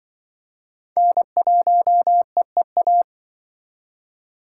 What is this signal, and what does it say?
Morse code, heard as a keyed tone, about 26 wpm, 700 Hz. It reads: N1EEA